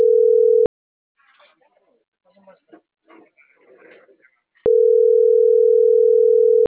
Ringback tone (North America), 440 + 480 Hz.